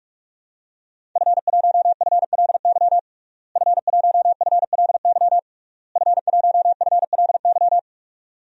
Morse code, heard as a keyed tone, about 46 wpm, 700 Hz. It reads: F1FLY F1FLY F1FLY
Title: DXCC-100 Call Signs - 5 Characters - Repeated 3x - 45wpm